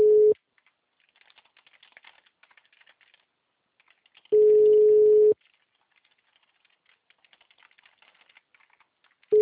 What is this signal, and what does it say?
Ringback tone (Europe, 425 Hz).